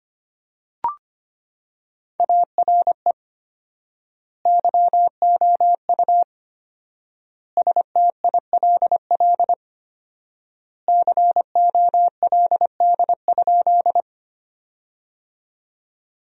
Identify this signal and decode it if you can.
Morse code, heard as a keyed tone, about 25 wpm, 700 Hz. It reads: ARE YOU STILL COLD?
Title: 4x Word-Spacing - Sentences from Top 400 Words - 25wpm